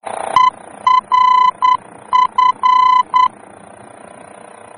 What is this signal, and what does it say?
Morse code, heard as a keyed tone, about 9 wpm, 1010 Hz. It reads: ERF
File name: germany / mx / ERF-D-VOR